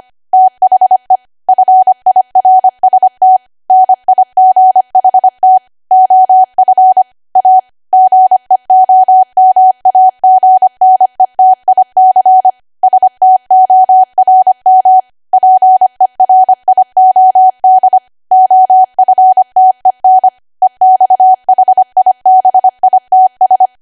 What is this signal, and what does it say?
Morse code, heard as a keyed tone, about 25 wpm, 750 Hz. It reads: THE FIRST NIGHT OF A GEOMAGNETIC STORM PERIOD OFTEN EXHIBITS